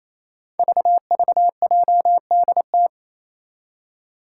Morse code, heard as a keyed tone, about 27 wpm, 700 Hz. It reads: VVJDT
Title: Five Challenging Letter-Number Combinations 28wpm